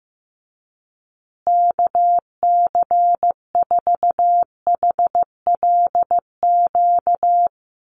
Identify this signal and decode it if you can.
Morse code, heard as a keyed tone, about 15 wpm, 700 Hz. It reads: KC4HLQ
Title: USA Call Signs Repeated 3x 15wpm